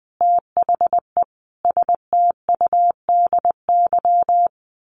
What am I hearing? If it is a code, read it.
Morse code, heard as a keyed tone, about 20 wpm, 700 Hz. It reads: THE STUDY
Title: Sentences from Top 900 Words 20wpm